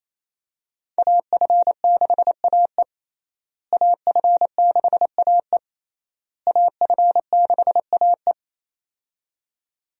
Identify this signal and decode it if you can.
Morse code, heard as a keyed tone, about 28 wpm, 700 Hz. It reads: AF6AE AF6AE AF6AE
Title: USA Call Signs - 5 Characters - Repeated 3x - 28wpm